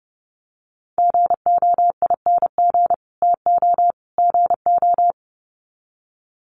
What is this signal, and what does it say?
Morse code, heard as a keyed tone, about 30 wpm, 700 Hz. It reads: GOING TO GO
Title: Sets of 3 Words 30wpm